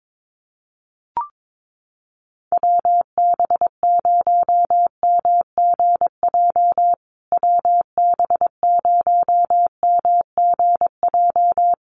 Morse code, heard as a keyed tone, about 22 wpm, 700 Hz. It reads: WB0MGJ WB0MGJ
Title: USA Call Signs Repeated 2x 22wpm